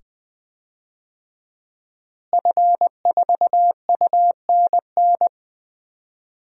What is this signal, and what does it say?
Morse code, heard as a keyed tone, about 20 wpm, 700 Hz. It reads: F4UNN
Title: DXCC-100 Call Signs - 5 Characters - Repeated 2x - 20wpm